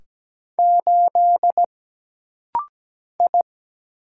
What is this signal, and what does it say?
Morse code, heard as a keyed tone, about 17 wpm, 700 Hz. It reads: 8 I
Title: Single Letter-Number - Rapid-Fire 17wpm